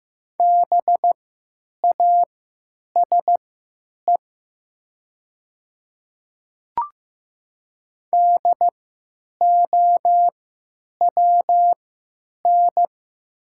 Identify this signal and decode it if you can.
Morse code, heard as a keyed tone, about 15 wpm, spaced as for 9 wpm, 700 Hz. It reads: BASE DOWN
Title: 3x Character Spacing - Top 300 Words - 15wpm